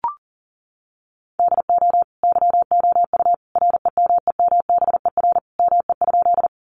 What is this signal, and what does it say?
Morse code, heard as a keyed tone, about 37 wpm, 700 Hz. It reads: DO YOU REMEMBER ME?
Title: Sentences from Top 400 Words 40wpm